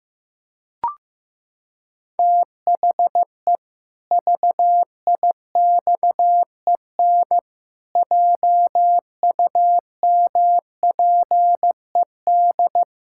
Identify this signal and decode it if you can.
Morse code, heard as a keyed tone, about 15 wpm, 700 Hz. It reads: THE VIXEN JUMPED
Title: Pangram Challenge v2 - 15wpm